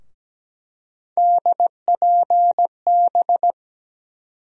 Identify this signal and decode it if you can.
Morse code, heard as a keyed tone, about 17 wpm, 700 Hz. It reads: DPB
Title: USA Call Sign Postfixes - 3 Letters 17wpm